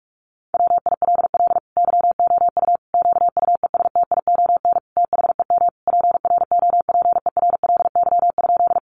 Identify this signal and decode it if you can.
Morse code, heard as a keyed tone, about 44 wpm, 700 Hz. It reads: WILL YOU QUESTION THEM PROPERLY?
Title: Sentences from Top 1100 Words 45wpm